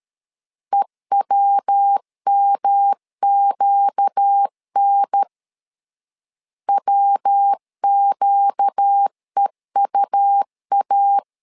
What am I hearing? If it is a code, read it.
Morse code, heard as a keyed tone, about 13 wpm, 800 Hz. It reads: EWMQN WQEUA
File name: LEZIONI / 4 Q 12wpm 1833